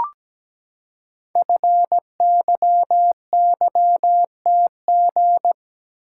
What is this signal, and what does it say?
Morse code, heard as a keyed tone, about 17 wpm, 700 Hz. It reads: FYYTG
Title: Code Groups - 5 Letters - 17wpm